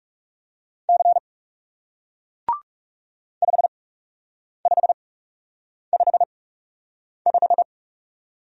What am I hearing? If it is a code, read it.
Morse code, heard as a keyed tone, about 40 wpm, spaced as for 8 wpm, 700 Hz. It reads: C 5555